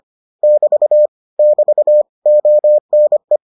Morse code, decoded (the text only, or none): <BT> <BT> ONE